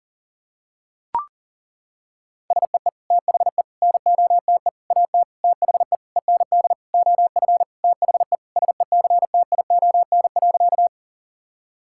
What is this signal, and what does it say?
Morse code, heard as a keyed tone, about 37 wpm, 700 Hz. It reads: SEE THE NOTE AT THE END OF THE SECTION.
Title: Sentences from Top 1000 Words - Review Part 5 - 40wpm